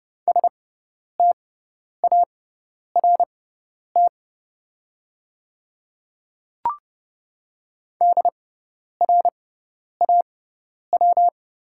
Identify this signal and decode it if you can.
Morse code, heard as a keyed tone, about 32 wpm, spaced as for 10 wpm, 700 Hz. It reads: START DRAW
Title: 6x Character Spacing - Top 300 Words - 30wpm